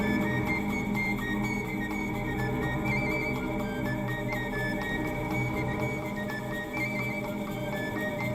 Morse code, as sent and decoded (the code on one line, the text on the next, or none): none